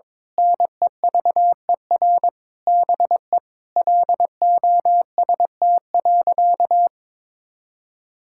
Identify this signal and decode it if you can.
Morse code, heard as a keyed tone, about 22 wpm, 700 Hz. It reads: NEVER BE LOST.
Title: Sentences from Top 1600 Words 22wpm